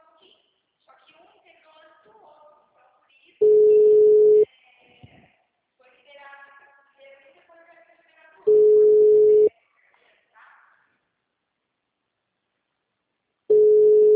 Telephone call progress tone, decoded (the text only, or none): ringback tone